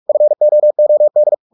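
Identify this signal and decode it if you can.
Morse code, heard as a keyed tone, about 44 wpm, 600 Hz. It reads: FOOD
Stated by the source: Morse practice at 45 wpm, the novel Never by Ken Follett in English, from CW med Gnister.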